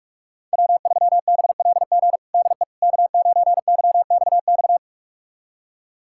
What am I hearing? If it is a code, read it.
Morse code, heard as a keyed tone, about 46 wpm, 700 Hz. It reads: W3BLG DE K9YXX
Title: USA Callsign DE Callsign - 5 Characters - 45wpm